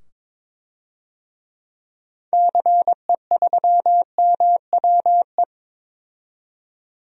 Morse code, heard as a keyed tone, about 22 wpm, 700 Hz. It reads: CE3MWE
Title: DXCC-100 Call Signs - 6 Characters - 22wpm